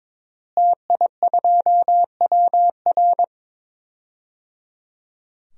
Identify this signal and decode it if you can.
Morse code, heard as a keyed tone, about 22 wpm, 700 Hz. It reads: TI2WR
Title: DXCC-100 Call Signs - 5 Characters - Repeated 3x - 22wpm